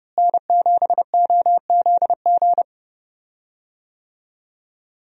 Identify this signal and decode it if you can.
Morse code, heard as a keyed tone, about 30 wpm, 700 Hz. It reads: N7OZG
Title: USA Call Signs - 5 Characters - Repeated 3x - 30wpm